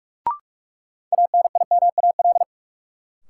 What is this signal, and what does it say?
Morse code, heard as a keyed tone, about 46 wpm, 700 Hz. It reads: ANIMAL
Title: Sentences from Top 1000 Words - Review Part 1 - New Word Drill - 45wpm